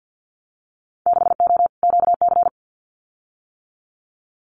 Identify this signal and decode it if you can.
Morse code, heard as a keyed tone, about 50 wpm, 700 Hz. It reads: 6O QC
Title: Winter Field Day Exchanges Encore 50wpm